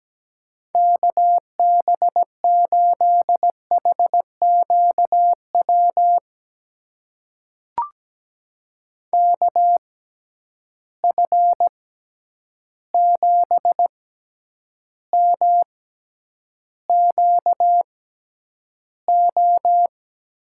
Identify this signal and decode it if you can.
Morse code, heard as a keyed tone, about 17 wpm, spaced as for 9 wpm, 700 Hz. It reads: KB8HQW KF7MQO
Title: USA Call Signs - 6 Characters - 6x Character Spacing - 17wpm